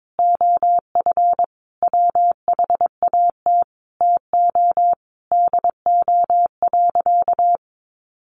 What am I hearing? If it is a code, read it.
Morse code, heard as a keyed tone, about 22 wpm, 700 Hz. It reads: OF WHAT TO DO.
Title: Sentences from Top 1400 Words 22wpm